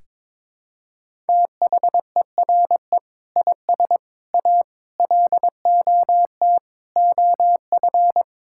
Morse code, heard as a keyed tone, about 21 wpm, 700 Hz. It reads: THERE IS A LOT OF
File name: practice / 22wpm / Sentences with 3-Word Phrases - 22wpm